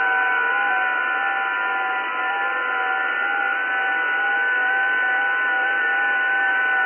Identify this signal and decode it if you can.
FT8.
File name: test / websdr test15 12k